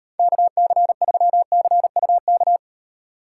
Morse code, heard as a keyed tone, about 38 wpm, 700 Hz. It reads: KC3CUK